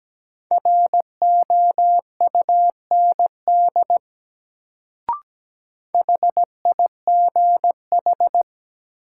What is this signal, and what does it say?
Morse code, heard as a keyed tone, about 17 wpm, 700 Hz. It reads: ROUND HIGH
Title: Sentences from Top 200 Words - New Word Drill - 17wpm